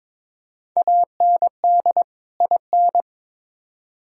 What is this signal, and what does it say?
Morse code, heard as a keyed tone, about 21 wpm, 700 Hz. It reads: AND IN
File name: Build A Phrase - 2-Word Phrases - 22wpm